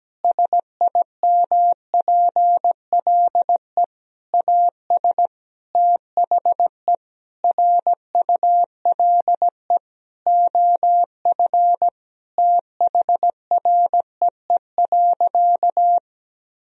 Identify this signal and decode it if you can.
Morse code, heard as a keyed tone, about 17 wpm, 700 Hz. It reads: SIMPLE AS THE RULE OF THREE.
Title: Sentences from Top 1000 Words - Review Part 2 - 17wpm